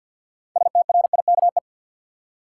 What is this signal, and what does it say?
Morse code, heard as a keyed tone, about 48 wpm, 700 Hz. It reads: STRIKE
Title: Six Letter Words - 50wpm